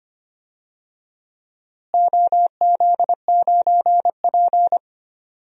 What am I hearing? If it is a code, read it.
Morse code, heard as a keyed tone, about 25 wpm, 700 Hz. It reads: OZ9P
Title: DXCC-100 Call Signs - 4 Characters - Repeated 3x - 25wpm